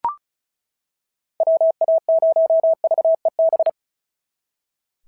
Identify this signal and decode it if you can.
Morse code, heard as a keyed tone, about 36 wpm, 690 Hz. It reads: WA0VEB
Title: SOTA QSO Practice 35wpm